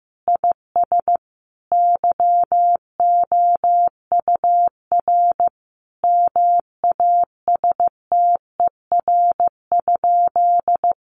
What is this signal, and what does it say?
Morse code, heard as a keyed tone, about 15 wpm, 700 Hz. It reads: IS YOUR MASTER?
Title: Sentences from Top 1300 Words 15wpm